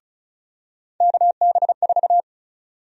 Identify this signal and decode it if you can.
Morse code, heard as a keyed tone, about 34 wpm, 700 Hz. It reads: KB4
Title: USA Call Sign Prefixes - 35wpm